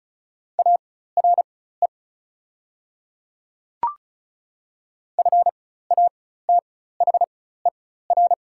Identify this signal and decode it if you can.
Morse code, heard as a keyed tone, about 34 wpm, spaced as for 16 wpm, 700 Hz. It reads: ARE FATHER
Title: 4x Character Spacing - Top 200 Words - 35wpm